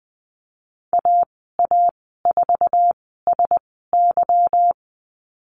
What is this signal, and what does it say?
Morse code, heard as a keyed tone, about 20 wpm, spaced as for 15 wpm, 700 Hz. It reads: AA4SY
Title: 173 Lesson 28g - Number 9 - Call Signs 20wpm